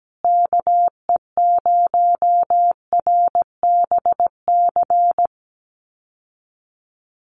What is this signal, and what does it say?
Morse code, heard as a keyed tone, about 17 wpm, 700 Hz. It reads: KE0RBC
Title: USA Call Signs Repeated 2x 17wpm